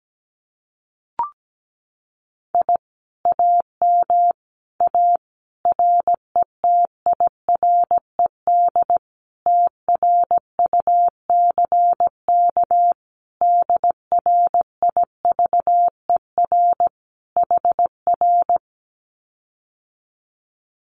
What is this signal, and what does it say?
Morse code, heard as a keyed tone, about 17 wpm, 700 Hz. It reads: I AM A RETIRED TRUCK DRIVER HR
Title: Common QSO Phrases Encore 17wpm